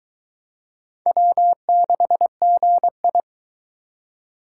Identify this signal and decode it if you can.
Morse code, heard as a keyed tone, about 22 wpm, 700 Hz. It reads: W6GI